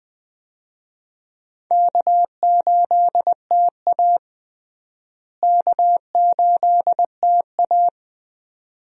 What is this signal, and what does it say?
Morse code, heard as a keyed tone, about 21 wpm, 700 Hz. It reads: K8TA K8TA
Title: USA Call Signs - 4 Characters - Repeated 2x - 20wpm